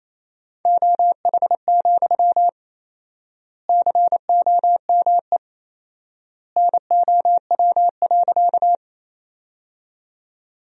Morse code, heard as a keyed tone, about 28 wpm, 700 Hz. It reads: OH, COME NOW.
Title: 4x Word-Spacing - Sentences from Top 500 Words - 28wpm